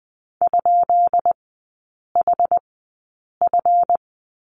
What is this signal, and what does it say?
Morse code, heard as a keyed tone, about 20 wpm, spaced as for 8 wpm, 700 Hz. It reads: ?HF
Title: 172 Lesson 28e - Number 9 - Three Characters 20wpm